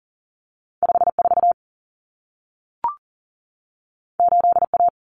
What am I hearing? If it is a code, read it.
Morse code, heard as a keyed tone, about 40 wpm, 700 Hz. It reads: 54 8A
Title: Common Cut Numbers - 2-Digit 40wpm